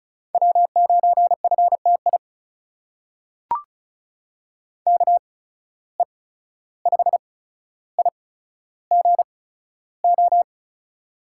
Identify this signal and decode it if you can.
Morse code, heard as a keyed tone, about 36 wpm, spaced as for 14 wpm, 700 Hz. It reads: W9FTI KE5IGO
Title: USA Call Signs - 8x Character Spacing - 35wpm